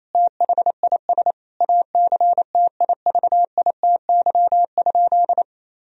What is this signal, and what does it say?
Morse code, heard as a keyed tone, about 28 wpm, 700 Hz. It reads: THIS ACTIVITY?
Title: Sentences from Top 1000 Words - Review Part 4 - 28wpm